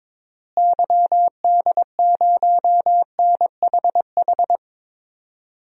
Morse code, heard as a keyed tone, about 22 wpm, 700 Hz. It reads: YD0NHH
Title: DXCC-100 Call Signs - 6 Characters - Repeated 2x - 22wpm